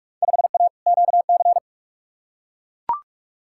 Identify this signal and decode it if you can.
Morse code, heard as a keyed tone, about 42 wpm, 700 Hz. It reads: 5A QC
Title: Field Day Exchanges Encore - 45wpm